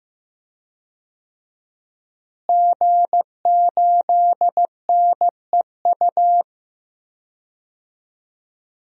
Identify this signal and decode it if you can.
Morse code, heard as a keyed tone, about 15 wpm, 700 Hz. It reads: G8NEU
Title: DXCC-100 Call Signs - 5 Characters - Repeated 2x - 15wpm